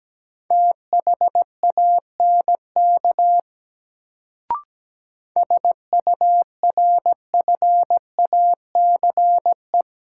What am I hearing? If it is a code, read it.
Morse code, heard as a keyed tone, about 17 wpm, 700 Hz. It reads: THANK SURFACE
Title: Sentences from Top 600 Words - New Word Drill - 17wpm